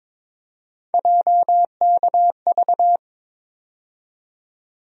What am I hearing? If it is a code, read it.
Morse code, heard as a keyed tone, about 22 wpm, 700 Hz. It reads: JKV
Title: USA Call Sign Postfixes - 3 Letters 22wpm